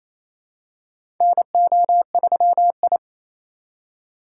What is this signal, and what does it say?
Morse code, heard as a keyed tone, about 28 wpm, 700 Hz. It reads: NO3I
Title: USA Call Signs - 4 Characters - 2x Character Spacing - 28wpm